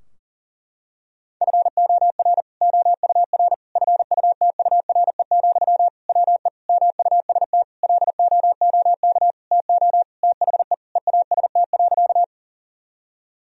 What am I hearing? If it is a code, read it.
Morse code, heard as a keyed tone, about 40 wpm, 700 Hz. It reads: FOR OUR FUTURE, WE MUST LOOK TO THE EAST.
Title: Sentences from Top 1000 Words - Review Part 3 - 40wpm